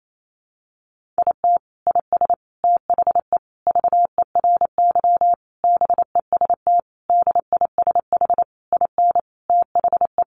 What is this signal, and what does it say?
Morse code, heard as a keyed tone, about 28 wpm, 700 Hz. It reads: IT IS THE VERY BEST DISH IN THE
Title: Sentences from Top 1300 Words 28wpm